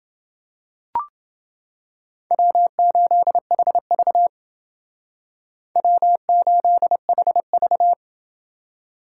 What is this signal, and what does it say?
Morse code, heard as a keyed tone, about 29 wpm, 700 Hz. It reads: W8HV W8HV